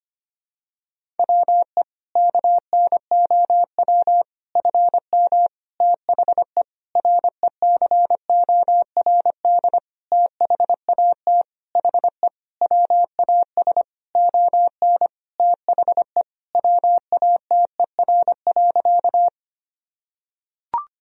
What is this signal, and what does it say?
Morse code, heard as a keyed tone, about 25 wpm, 700 Hz. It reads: WE KNOW FM THE RECORD THAT HE WAS ON THE WATER.
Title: Sentences from Top 500 Words and Abbreviations 25wpm